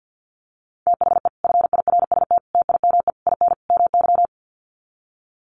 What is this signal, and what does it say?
Morse code, heard as a keyed tone, about 48 wpm, 700 Hz. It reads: THE FIRST TIME IN MY
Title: Sets of 5 Words Encore 50wpm